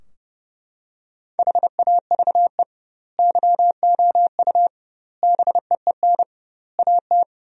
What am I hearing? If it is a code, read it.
Morse code, heard as a keyed tone, about 30 wpm, 700 Hz. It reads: HAVE YOU BEEN AT
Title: 2x Word-Spacing - Sentences from Top 300 Words - 30wpm